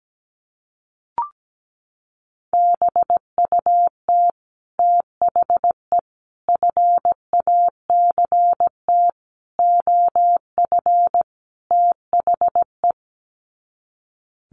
Morse code, heard as a keyed tone, about 17 wpm, 700 Hz. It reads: BUT THE FACT OF THE
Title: Sets of 5 Words Encore2 17wpm